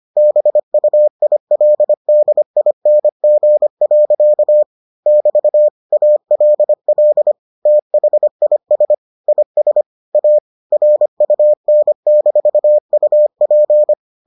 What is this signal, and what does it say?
Morse code, heard as a keyed tone, about 25 wpm, 600 Hz. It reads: BUILDING. <BT> ALL THIS IS A RUN-UP